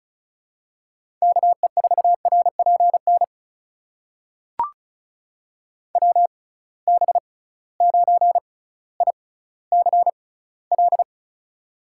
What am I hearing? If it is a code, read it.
Morse code, heard as a keyed tone, about 36 wpm, spaced as for 18 wpm, 700 Hz. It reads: KE4RPN WB9ICL
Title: USA Call Signs - 6 Characters - 6x Character Spacing - 35wpm